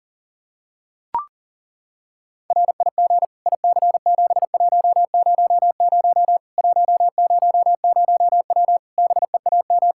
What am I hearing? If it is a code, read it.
Morse code, heard as a keyed tone, about 40 wpm, 700 Hz. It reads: RIG IC7100 100W BEAM
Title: Practice QSO - Exchange 2 - Info - 40wpm